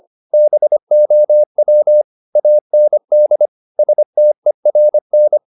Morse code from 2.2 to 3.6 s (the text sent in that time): AND